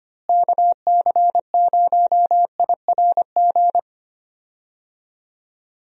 Morse code, heard as a keyed tone, about 25 wpm, 700 Hz. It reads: KC0IRG
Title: USA Call Signs - 6 Characters - Repeated 2x - 25wpm